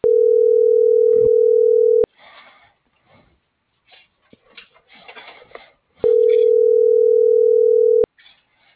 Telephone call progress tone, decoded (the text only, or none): ringback tone